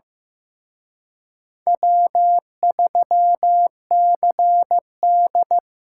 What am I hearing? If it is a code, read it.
Morse code, heard as a keyed tone, about 18 wpm, 700 Hz. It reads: W3CD
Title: Speed Racing - USA Call Signs 15wpm